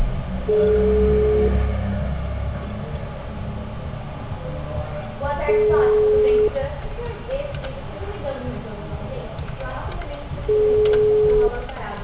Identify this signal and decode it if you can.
Ringback tone (Europe, 425 Hz), with speech.